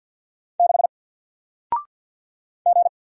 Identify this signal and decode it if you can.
Morse code, heard as a keyed tone, about 50 wpm, 700 Hz. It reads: 6 G
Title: Single Character (Letter:Number:Punctuation:Prosign) 50wpm